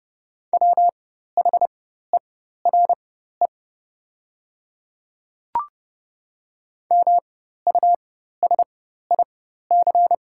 Morse code, heard as a keyed tone, about 30 wpm, spaced as for 14 wpm, 700 Hz. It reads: WHERE MUSIC
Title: 4x Character Spacing - Top 300 Words - 30wpm